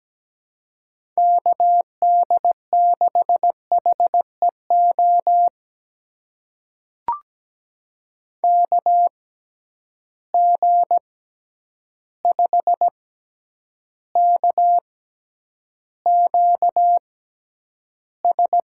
Morse code, heard as a keyed tone, about 17 wpm, spaced as for 9 wpm, 700 Hz. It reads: KD6HEO KG5KQS
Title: USA Call Signs - 6 Characters - 6x Character Spacing - 17wpm